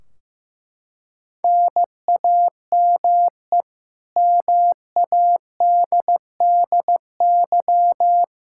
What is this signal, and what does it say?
Morse code, heard as a keyed tone, about 16 wpm, 700 Hz. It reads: NAME MADDY
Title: Practice QSO - Exchange 1 - Beginning - 15wpm